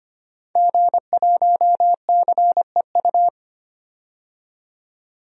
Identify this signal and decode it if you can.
Morse code, heard as a keyed tone, about 25 wpm, 700 Hz. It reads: G1CEU
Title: DXCC-100 Call Signs - 5 Characters - 25wpm